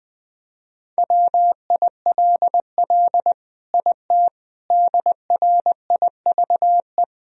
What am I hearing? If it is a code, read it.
Morse code, heard as a keyed tone, about 21 wpm, 700 Hz. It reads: WILL IT DRIVE
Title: Sentences from Top 1000 Words - Review Part 2 - 20wpm